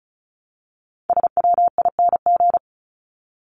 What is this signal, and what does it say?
Morse code, heard as a keyed tone, about 34 wpm, 700 Hz. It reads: SWING
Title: Top 1751-2000 Words 35wpm